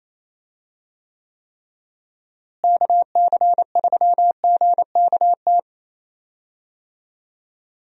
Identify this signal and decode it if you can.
Morse code, heard as a keyed tone, about 28 wpm, 700 Hz. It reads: KC3GKT
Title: USA Call Signs - 6 Characters - Repeated 3x - 28wpm